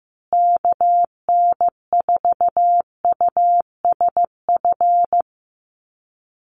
Morse code, heard as a keyed tone, about 14 wpm, 700 Hz. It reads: KN4USF
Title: USA Call Signs Repeated 2x 15wpm